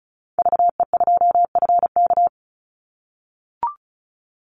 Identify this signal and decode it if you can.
Morse code, heard as a keyed tone, about 34 wpm, 700 Hz. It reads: VE2FK
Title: Call Signs - 35wpm